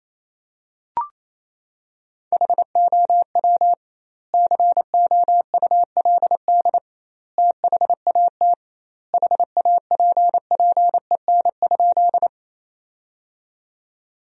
Morse code, heard as a keyed tone, about 27 wpm, 700 Hz. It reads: HOW COULD THAT HAPPEN?
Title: 2x Word-Spacing - Sentences from Top 400 Words - 28wpm